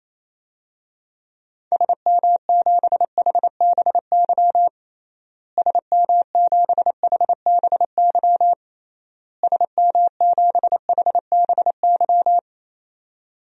Morse code, heard as a keyed tone, about 28 wpm, 700 Hz. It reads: SM7HBY SM7HBY SM7HBY
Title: DXCC-100 Call Signs - 6 Characters - Repeated 3x - 28wpm